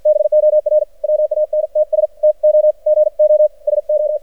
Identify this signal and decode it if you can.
Morse code, heard as a keyed tone, about 56 wpm, spaced as for 48 wpm, 600 Hz. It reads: BOR WANTS TO GO SO
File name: archived / CRSnet